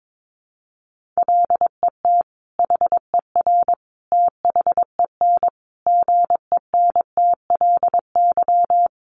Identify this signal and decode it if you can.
Morse code, heard as a keyed tone, about 22 wpm, 700 Hz. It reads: LET HER THEN GENTLY
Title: Sentences from Top 1500 Words 22wpm